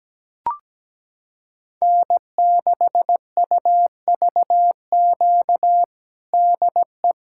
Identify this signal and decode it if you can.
Morse code, heard as a keyed tone, about 17 wpm, 700 Hz. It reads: N6UVQ DE
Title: USA Callsign DE Callsign - 5 Characters - 17wpm